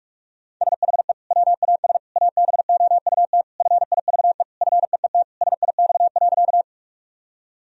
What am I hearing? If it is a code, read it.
Morse code, heard as a keyed tone, about 41 wpm, 700 Hz. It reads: SHE WAS ABOUT FIVE FEET SIX.
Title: Sentences from Top 1000 Words - Review Part 2 - 45wpm